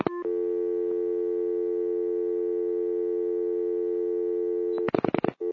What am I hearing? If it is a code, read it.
Dial tone (North America), 350 + 440 Hz.